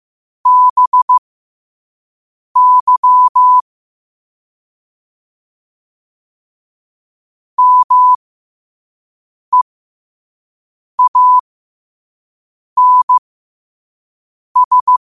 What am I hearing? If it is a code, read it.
Morse code, heard as a keyed tone, about 15 wpm, spaced as for 5 wpm, 1000 Hz. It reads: BY MEANS
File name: 5wpm